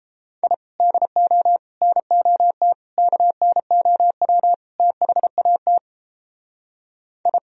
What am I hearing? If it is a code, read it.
Morse code, heard as a keyed tone, about 34 wpm, 700 Hz. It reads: I DO NOT KNOW THAT I
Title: Speed-Racing - Sets of 5 Words 28wpm